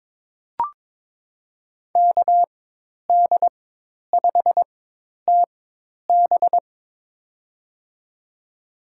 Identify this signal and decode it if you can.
Morse code, heard as a keyed tone, about 22 wpm, spaced as for 10 wpm, 700 Hz. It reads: KD5TB
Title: USA Call Signs - 5 Characters - 4x Character Spacing - 22wpm